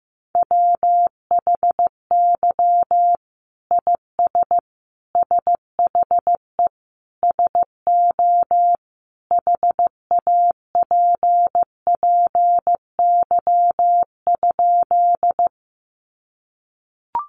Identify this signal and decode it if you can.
Morse code, heard as a keyed tone, about 15 wpm, 700 Hz. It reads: WHY IS SHE SO HAPPY?
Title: Sentences from Top 1000 Words 15wpm